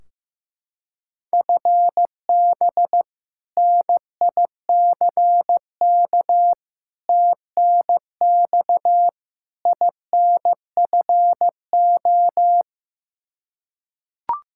Morse code, heard as a keyed tone, about 15 wpm, 700 Hz. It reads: FB NICK TNX INFO
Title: Practice QSO - Exchange 3 - More Info - 15wpm